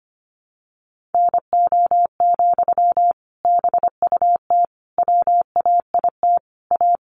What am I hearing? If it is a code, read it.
Morse code, heard as a keyed tone, about 25 wpm, 700 Hz. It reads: NO, BUT WAIT A
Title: Sentences from Top 900 Words 25wpm